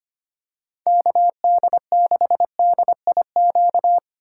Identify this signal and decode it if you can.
Morse code, heard as a keyed tone, about 29 wpm, 700 Hz. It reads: KD6DIQ
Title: Speed Racing - USA Call Signs 25wpm